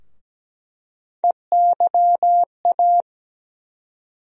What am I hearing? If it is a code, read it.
Morse code, heard as a keyed tone, about 17 wpm, 700 Hz. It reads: EYA
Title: USA Call Sign Postfixes - 3 Letters 17wpm